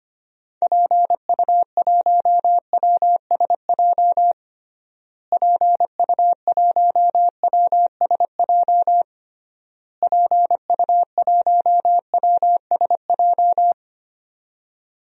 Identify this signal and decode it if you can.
Morse code, heard as a keyed tone, about 25 wpm, 700 Hz. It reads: PU1WSJ PU1WSJ PU1WSJ
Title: DXCC-100 Call Signs - 6 Characters - Repeated 3x - 25wpm